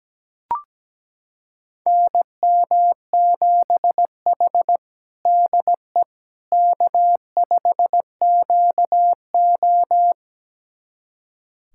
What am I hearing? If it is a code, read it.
Morse code, heard as a keyed tone, about 17 wpm, 700 Hz. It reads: NM7H DE K5QO
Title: USA Callsign DE Callsign - 4 Characters - 17wpm